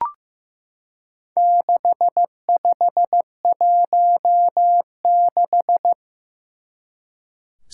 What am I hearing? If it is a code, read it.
Morse code, heard as a keyed tone, about 15 wpm, 700 Hz. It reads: 6516
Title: Four Challenging Letter-Number Combinations 15wpm